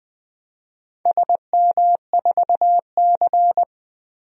Morse code, heard as a keyed tone, about 20 wpm, 700 Hz. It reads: SM4C
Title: DXCC-100 Call Signs - 4 Characters - Repeated 2x - 20wpm